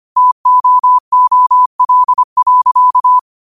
Morse code, heard as a keyed tone, about 25 wpm, 1000 Hz. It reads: TOOL.
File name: qotdfiles / 25wpm